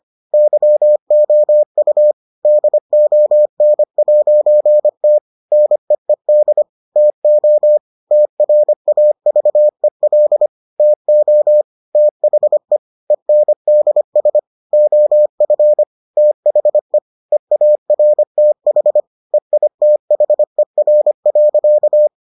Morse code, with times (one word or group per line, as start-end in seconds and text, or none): 0.3-2.1 YOU
2.5-5.2 DON'T
5.5-6.6 NEED
7.0-7.8 TO
8.1-10.5 TRAVEL
10.8-11.6 TO
12.0-12.8 THE
13.1-14.4 ENDS
14.7-15.8 OF
16.2-17.0 THE
17.3-19.0 EARTH
19.3-22.1 EITHER.